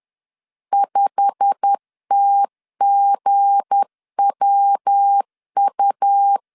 Morse code, heard as a keyed tone, about 10 wpm, 800 Hz. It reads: 5TGWU
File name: LEZIONI / 8 YJ45 / 8 YJ45 10wpm 2549